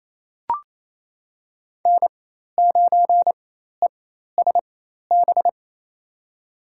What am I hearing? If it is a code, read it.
Morse code, heard as a keyed tone, about 28 wpm, spaced as for 13 wpm, 700 Hz. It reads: N9ESB
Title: USA Call Signs - 5 Characters - 4x Character Spacing - 28wpm